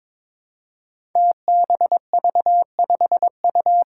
Morse code, heard as a keyed tone, about 22 wpm, 700 Hz. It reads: TBV5U
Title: Five Challenging Letter-Number Combinations 22wpm